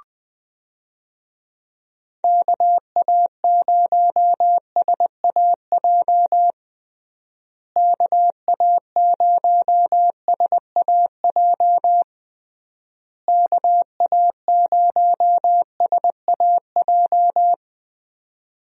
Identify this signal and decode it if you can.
Morse code, heard as a keyed tone, about 20 wpm, 700 Hz. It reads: KA0SAJ KA0SAJ KA0SAJ
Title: USA Call Signs - 6 Characters - Repeated 3x - 20wpm